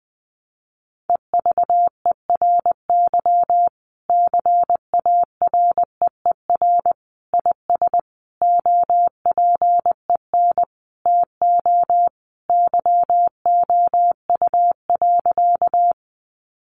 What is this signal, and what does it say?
Morse code, heard as a keyed tone, about 20 wpm, 700 Hz. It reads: EVERY CAREER IS OPEN TO YOU.
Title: Sentences from Top 1000 Words 20wpm